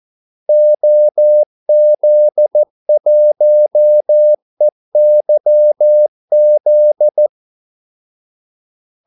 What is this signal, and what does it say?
Morse code, heard as a keyed tone, about 14 wpm, 600 Hz. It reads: OZ1EYZ
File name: Callsigns-14wpm